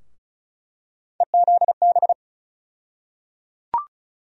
Morse code, heard as a keyed tone, about 34 wpm, 700 Hz. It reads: EZB
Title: USA Call Sign Postfixes - 3 Letters 35wpm